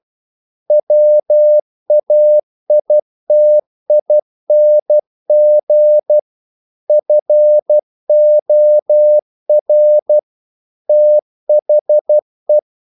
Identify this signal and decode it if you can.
Morse code, heard as a keyed tone, about 12 wpm, 600 Hz. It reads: WAITING FOR THE
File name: CW / Practice / EnglishQuotations0112